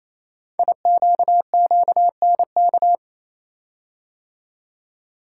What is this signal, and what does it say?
Morse code, heard as a keyed tone, about 27 wpm, 700 Hz. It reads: IQQNK
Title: Code Groups - 5 Letters - 28wpm